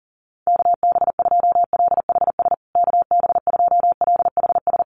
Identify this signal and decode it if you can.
Morse code, heard as a keyed tone, about 40 wpm, 700 Hz. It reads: KB2LHS KB2LHS
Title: USA Call Signs Repeated 3x 40wpm